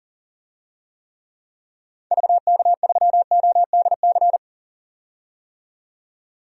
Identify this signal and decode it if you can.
Morse code, heard as a keyed tone, about 40 wpm, 700 Hz. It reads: VK3ODC
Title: DXCC-100 Call Signs - 6 Characters - 40wpm